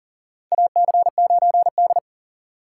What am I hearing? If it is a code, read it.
Morse code, heard as a keyed tone, about 40 wpm, 700 Hz. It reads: AC9D
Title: USA Call Signs - 2x Character Spacing - 40wpm